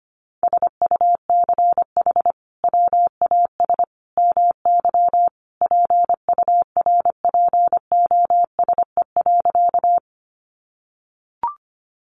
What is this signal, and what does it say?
Morse code, heard as a keyed tone, about 25 wpm, 700 Hz. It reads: SUCH WAS MY PURPOSE.
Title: Sentences from Top 800 Words 25wpm